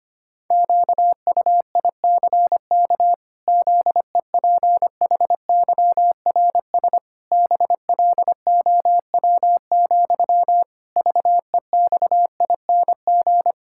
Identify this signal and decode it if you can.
Morse code, heard as a keyed tone, about 25 wpm, 700 Hz. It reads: QUICK ZEPHYRS BLOW, VEXING
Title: Pangram Challenge v2 - 25wpm